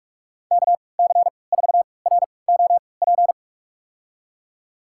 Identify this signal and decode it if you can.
Morse code, heard as a keyed tone, about 44 wpm, spaced as for 25 wpm, 700 Hz. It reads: KC4ROP
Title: USA Call Signs - 6 Characters - 3x Character Spacing - 45wpm